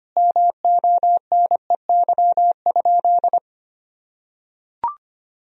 Morse code, heard as a keyed tone, about 25 wpm, 700 Hz. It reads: MONEY?
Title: Sentences from Top 1000 Words - Review Part 4 - 25wpm